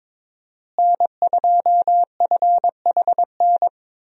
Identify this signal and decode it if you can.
Morse code, heard as a keyed tone, about 22 wpm, 700 Hz. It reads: N2FHN
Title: USA Call Signs - 5 Characters - Repeated 2x - 22wpm